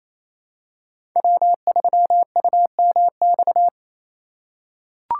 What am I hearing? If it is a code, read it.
Morse code, heard as a keyed tone, about 28 wpm, 700 Hz. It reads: W3UMX
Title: USA Call Signs - 5 Characters - 2x Character Spacing - 28wpm